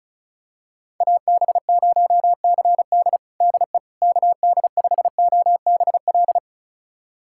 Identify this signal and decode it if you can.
Morse code, heard as a keyed tone, about 34 wpm, 700 Hz. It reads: AB0CD DE KD5OBL
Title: USA Callsign DE Callsign 35wpm